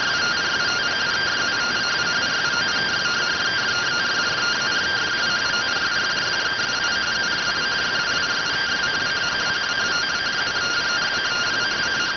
Two-tone teleprinter signal, about 170 Hz shift.